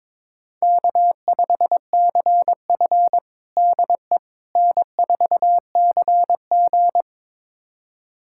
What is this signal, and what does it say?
Morse code, heard as a keyed tone, about 22 wpm, 700 Hz. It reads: K5CF DE N4CG
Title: USA Callsign DE Callsign - 4 Characters - 22wpm